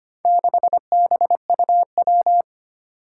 Morse code, heard as a keyed tone, about 25 wpm, 700 Hz. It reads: 6BUW
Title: Four Challenging Letter-Number Combinations 25wpm